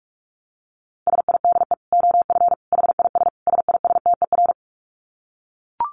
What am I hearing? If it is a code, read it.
Morse code, heard as a keyed tone, about 44 wpm, 700 Hz. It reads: SIDE OF HIS SISTER
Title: Dit Challenge 45wpm